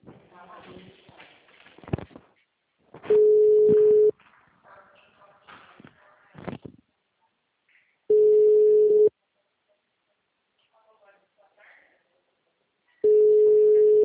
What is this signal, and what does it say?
Ringback tone (Europe, 425 Hz), with speech.